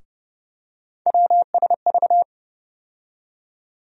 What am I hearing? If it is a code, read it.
Morse code, heard as a keyed tone, about 30 wpm, 700 Hz. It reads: WSV
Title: USA Call Sign Postfixes - 3 Letters 30wpm